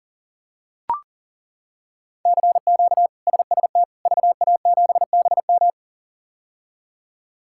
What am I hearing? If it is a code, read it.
Morse code, heard as a keyed tone, about 40 wpm, 700 Hz. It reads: CQ SST VA7BM
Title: K1USN SST QSO Practice - 40wpm